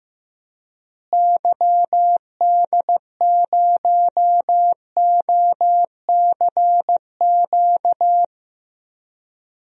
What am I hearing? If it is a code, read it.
Morse code, heard as a keyed tone, about 15 wpm, 700 Hz. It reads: YD0OCQ
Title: DXCC-100 Call Signs - 6 Characters - Repeated 3x - 15wpm